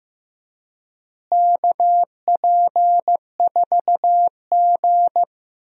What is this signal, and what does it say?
Morse code, heard as a keyed tone, about 15 wpm, 700 Hz. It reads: KP4G
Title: DXCC-100 Call Signs - 4 Characters - Repeated 2x - 15wpm